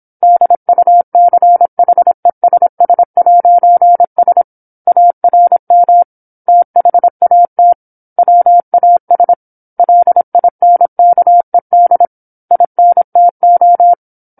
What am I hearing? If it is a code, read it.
Morse code, heard as a keyed tone, about 27 wpm, 700 Hz. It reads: DUCHESS'S ARM THAT WAS LINKED INTO
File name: 08-Chapter-08 26 WPM